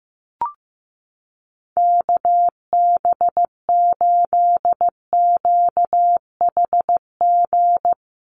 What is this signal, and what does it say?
Morse code, heard as a keyed tone, about 14 wpm, 700 Hz. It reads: KB8QHG
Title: USA Call Signs Repeated 2x 15wpm